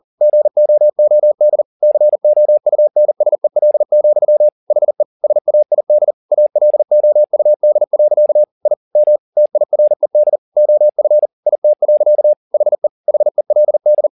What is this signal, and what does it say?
Morse code, heard as a keyed tone, about 40 wpm, 600 Hz. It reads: GOOD COUNSEL, HE SAID ALOUD. I M TIRED OF IT. HE HELD
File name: the old man and the sea 2 40wpm